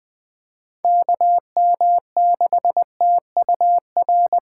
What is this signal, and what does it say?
Morse code, heard as a keyed tone, about 20 wpm, 700 Hz. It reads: KM6TUR
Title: USA Callsign DE Callsign 20wpm